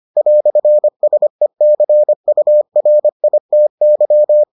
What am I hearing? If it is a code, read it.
Morse code, heard as a keyed tone, about 25 wpm, 600 Hz. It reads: "SECURITY